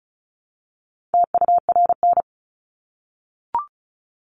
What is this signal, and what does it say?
Morse code, heard as a keyed tone, about 34 wpm, 700 Hz. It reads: TURN
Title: Top 500 Words 35wpm